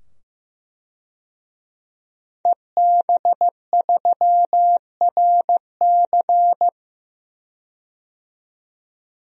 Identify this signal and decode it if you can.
Morse code, heard as a keyed tone, about 15 wpm, 700 Hz. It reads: EB3RC
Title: DXCC-100 Call Signs - 5 Characters - 15wpm